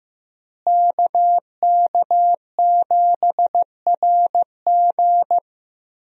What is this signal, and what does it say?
Morse code, heard as a keyed tone, about 15 wpm, 700 Hz. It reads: KK7RG
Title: USA Call Signs - 5 Characters - Repeated 3x - 15wpm